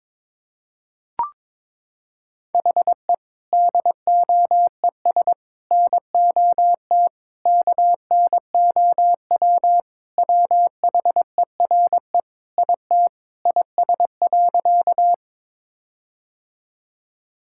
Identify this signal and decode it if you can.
Morse code, heard as a keyed tone, about 22 wpm, 700 Hz. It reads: HE DOES NOT KNOW WHERE IT IS.
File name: practice / 22wpm / Sentences with 4-Word Phrases - 22wpm